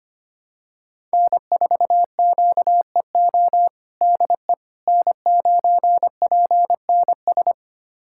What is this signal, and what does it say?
Morse code, heard as a keyed tone, about 25 wpm, 700 Hz. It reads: N4QEO DE N9PNS
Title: USA Callsign DE Callsign - 5 Characters - 25wpm